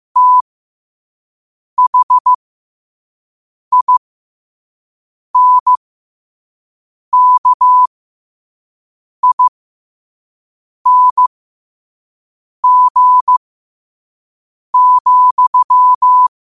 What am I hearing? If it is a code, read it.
Morse code, heard as a keyed tone, about 15 wpm, spaced as for 5 wpm, 1000 Hz. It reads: THINKING,
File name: qotdfiles / 5wpm